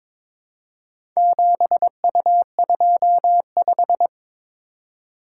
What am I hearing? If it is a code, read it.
Morse code, heard as a keyed tone, about 22 wpm, 700 Hz. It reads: 7U25
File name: practice / 22wpm / Four Letter-Number Combinations - 22wpm